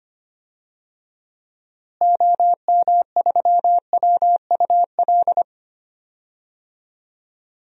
Morse code, heard as a keyed tone, about 25 wpm, 700 Hz. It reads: OM3WUL
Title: DXCC-100 Call Signs - 6 Characters - Repeated 3x - 25wpm